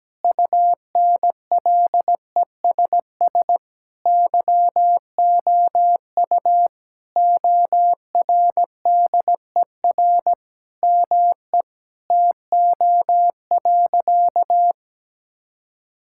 Morse code, heard as a keyed tone, about 17 wpm, 700 Hz. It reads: UNLESS YOU ORDER ME TO.
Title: Sentences from Top 1000 Words - Review Part 6 - 17wpm